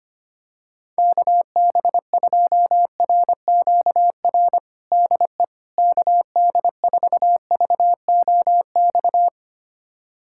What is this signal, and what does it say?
Morse code, heard as a keyed tone, about 25 wpm, 700 Hz. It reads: KB2RQR DE KD4VOX
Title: USA Callsign DE Callsign - 6 Characters - 25wpm